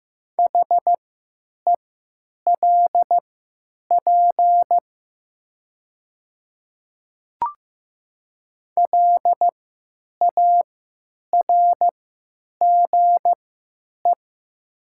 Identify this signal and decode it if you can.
Morse code, heard as a keyed tone, about 15 wpm, spaced as for 9 wpm, 700 Hz. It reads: HELP LARGE
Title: 3x Character Spacing - Top 500 Words - 15wpm